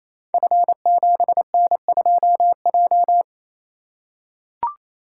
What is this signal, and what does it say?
Morse code, heard as a keyed tone, about 27 wpm, 700 Hz. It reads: F7N2J
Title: Code Groups - 5 Letter:Numbers - 28wpm